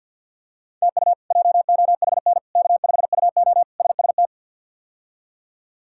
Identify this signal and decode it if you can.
Morse code, heard as a keyed tone, about 46 wpm, 700 Hz. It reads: TU JOHN K5UO SST
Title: K1USN SST QSO Practice - 50wpm